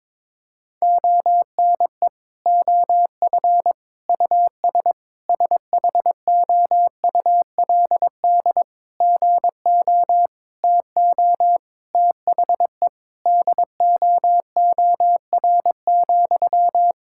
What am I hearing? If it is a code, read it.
Morse code, heard as a keyed tone, about 22 wpm, 700 Hz. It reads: ONE OF US SHOULD GO TO THE DOOR,
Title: Sentences from Top 1600 Words 22wpm